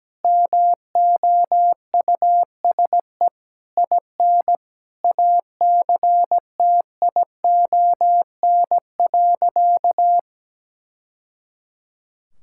Morse code, heard as a keyed tone, about 17 wpm, 700 Hz. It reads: MOUSE IN ACTION.
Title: Sentences from Top 1000 Words - Review Part 5 - 17wpm